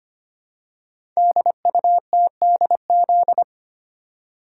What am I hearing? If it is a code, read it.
Morse code, heard as a keyed tone, about 25 wpm, 700 Hz. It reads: DUTDZ
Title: Code Groups - 5 Letters - 25wpm